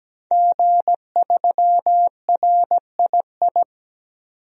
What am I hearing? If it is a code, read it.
Morse code, heard as a keyed tone, about 17 wpm, 700 Hz. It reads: G3RII
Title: DXCC-100 Call Signs - 5 Characters - 17wpm